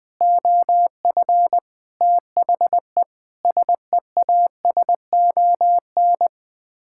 Morse code, heard as a keyed tone, about 20 wpm, 700 Hz. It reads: OF THE SEASON
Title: Sets of 6 Words - 20wpm